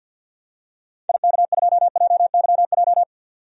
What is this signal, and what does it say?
Morse code, heard as a keyed tone, about 50 wpm, 700 Hz. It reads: IK2JYJ
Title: DXCC-100 Call Signs - 6 Characters - 50wpm